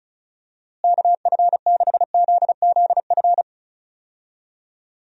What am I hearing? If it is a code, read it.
Morse code, heard as a keyed tone, about 36 wpm, 700 Hz. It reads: KF6ZZF
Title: USA Call Signs - 6 Characters - Repeated 3x - 35wpm